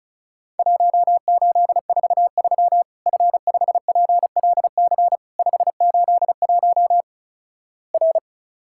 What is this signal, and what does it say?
Morse code, heard as a keyed tone, about 36 wpm, 700 Hz. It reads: 1843 F5PLC 581 R
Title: Worked All Europe DX Contest - QTC Exchange - 35wpm